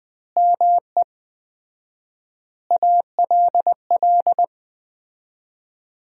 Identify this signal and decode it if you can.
Morse code, heard as a keyed tone, about 20 wpm, 700 Hz. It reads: ME ALL
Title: 4x Word-Spacing - Sentences from Top 200 Words - 20wpm